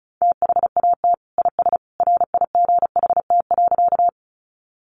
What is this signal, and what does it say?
Morse code, heard as a keyed tone, about 34 wpm, 700 Hz. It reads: THAT IS RIGHT.
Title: Sentences from Top 200 Words 35wpm